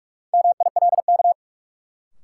Morse code, heard as a keyed tone, about 41 wpm, 700 Hz. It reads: MILK
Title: Sentences from Top 1500 Words - New Word Drill - 45wpm